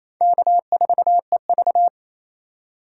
Morse code, heard as a keyed tone, about 28 wpm, 700 Hz. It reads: K4EV
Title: USA Call Signs - 4 Characters - 3x Character Spacing - 28wpm